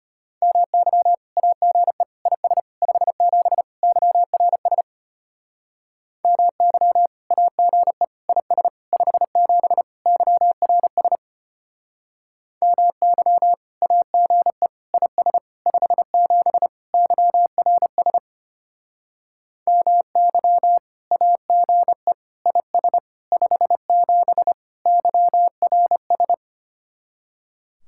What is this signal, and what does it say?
Morse code, heard as a keyed tone, about 32 wpm, 700 Hz. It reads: MY AGE IS 57 YRS MY AGE IS 57 YRS MY AGE IS 57 YRS MY AGE IS 57 YRS